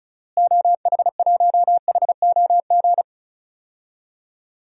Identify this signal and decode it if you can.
Morse code, heard as a keyed tone, about 34 wpm, 700 Hz. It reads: OH1HOG
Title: DXCC-100 Call Signs - 6 Characters - Repeated 2x - 35wpm